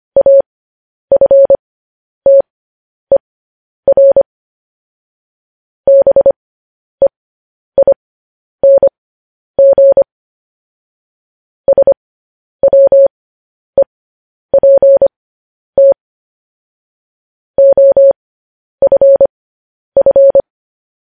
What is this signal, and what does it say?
Morse code, heard as a keyed tone, about 25 wpm, spaced as for 10 wpm, 550 Hz. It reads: AFTER BEING SWEPT OFF
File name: podcast / News headlines-1x-25x10